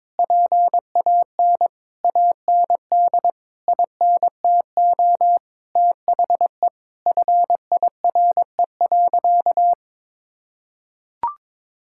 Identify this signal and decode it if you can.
Morse code, heard as a keyed tone, about 21 wpm, 700 Hz. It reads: PAN AND INTO THE FIRE.
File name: Sentences with 9-Word Phrases - 22wpm